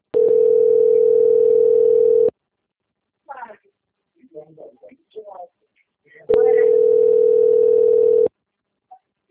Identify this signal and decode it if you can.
Ringback tone (North America), 440 + 480 Hz, with speech.
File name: rg-1600-unknown- - - .18369